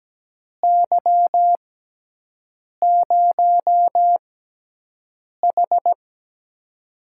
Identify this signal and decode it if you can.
Morse code, heard as a keyed tone, about 17 wpm, spaced as for 6 wpm, 700 Hz. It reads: Y0H
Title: USA Call Signs - 4 Characters - 6x Character Spacing - 17wpm